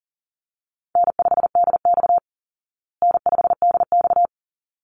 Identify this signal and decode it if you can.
Morse code, heard as a keyed tone, about 40 wpm, 700 Hz. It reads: N5DX N5DX
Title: Call Signs Repeated 3x - 40wpm